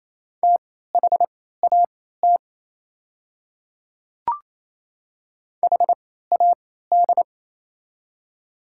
Morse code, heard as a keyed tone, about 28 wpm, spaced as for 16 wpm, 700 Hz. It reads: THAT HAD
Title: 3x Character Spacing - Top 100 Words - 28wpm